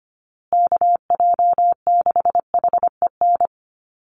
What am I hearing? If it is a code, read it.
Morse code, heard as a keyed tone, about 25 wpm, 700 Hz. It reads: KJ6HEN
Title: USA Call Signs Repeated 2x 25wpm